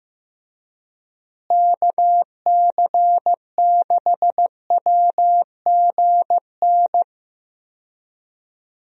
Morse code, heard as a keyed tone, about 15 wpm, 700 Hz. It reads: KC6WGN
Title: USA Call Signs - 6 Characters - Repeated 2x - 15wpm